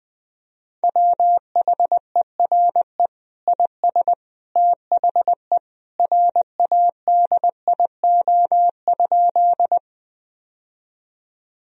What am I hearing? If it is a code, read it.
Morse code, heard as a keyed tone, about 21 wpm, 700 Hz. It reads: WHERE IS THE RADIO?
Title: Sentences from Top 1000 Words - Review Part 3 - 20wpm